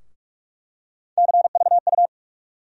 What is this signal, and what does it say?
Morse code, heard as a keyed tone, about 46 wpm, 700 Hz. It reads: CVU
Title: USA Call Sign Postfixes - 3 Letters 45wpm